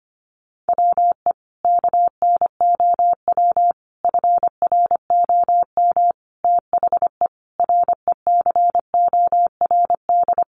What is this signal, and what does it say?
Morse code, heard as a keyed tone, about 25 wpm, 700 Hz. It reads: WE KNOW FROM THE RECORD
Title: Sentences from Top 500 Words 25wpm